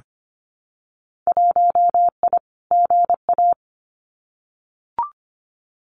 Morse code, heard as a keyed tone, about 25 wpm, 700 Hz. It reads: 1I GA
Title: Winter Field Day Exchanges Encore 25wpm